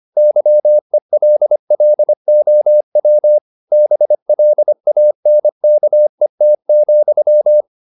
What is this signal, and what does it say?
Morse code, heard as a keyed tone, about 25 wpm, 600 Hz. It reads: YELLOW BLANKET,